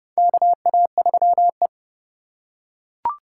Morse code, heard as a keyed tone, about 30 wpm, 700 Hz. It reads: KA3E
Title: USA Call Signs - 4 Characters - 4x Character Spacing - 30wpm